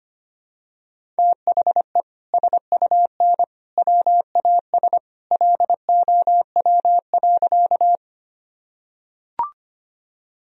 Morse code, heard as a keyed tone, about 25 wpm, 700 Hz. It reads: THE SUN WAS LOW.
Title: Sentences from Top 1000 Words - Review Part 1 - 25wpm